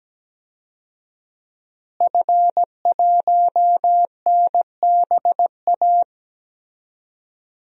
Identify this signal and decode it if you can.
Morse code, heard as a keyed tone, about 17 wpm, 700 Hz. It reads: F1NBA
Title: DXCC-100 Call Signs - 5 Characters - Repeated 2x - 17wpm